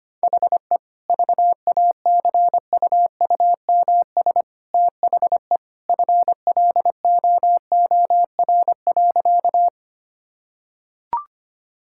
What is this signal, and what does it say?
Morse code, heard as a keyed tone, about 25 wpm, 690 Hz. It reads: HE VACUUMS THE FLOOR.
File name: Easy Sentences with Hints - 25wpm